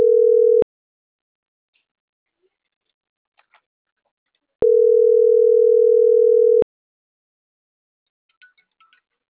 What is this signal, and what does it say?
Ringback tone (North America), 440 + 480 Hz.